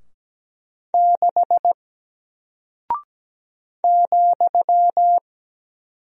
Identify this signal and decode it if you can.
Morse code, heard as a keyed tone, about 17 wpm, 700 Hz. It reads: 6 ,
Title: Single Character (Letter:Number:Punctuation:Prosign) 17wpm